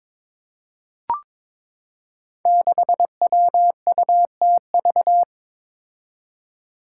Morse code, heard as a keyed tone, about 22 wpm, 700 Hz. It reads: 6WUTV
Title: Five Challenging Letter-Number Combinations 22wpm